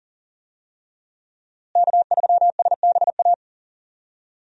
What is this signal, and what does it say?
Morse code, heard as a keyed tone, about 37 wpm, 700 Hz. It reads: K3SBA
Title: USA Call Signs - 5 Characters - Repeated 2x - 40wpm